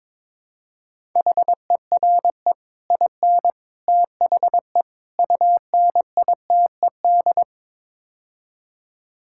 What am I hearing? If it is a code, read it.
Morse code, heard as a keyed tone, about 22 wpm, 700 Hz. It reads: HERE IN THE UNITED
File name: practice / 22wpm / Build A Phrase - 5-Word Phrases - 22wpm